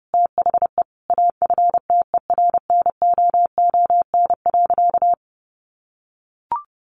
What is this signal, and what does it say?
Morse code, heard as a keyed tone, about 30 wpm, 700 Hz. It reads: THE AFTERNOON.
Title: Sentences from Top 1500 Words 30wpm